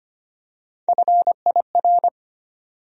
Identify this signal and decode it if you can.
Morse code, heard as a keyed tone, about 25 wpm, 700 Hz. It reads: FIR
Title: USA Call Sign Postfixes - 3 Letters 25wpm